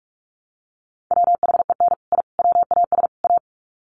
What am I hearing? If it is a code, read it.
Morse code, heard as a keyed tone, about 44 wpm, 700 Hz. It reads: WHEN I WAS A
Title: Sets of 4 Words 45wpm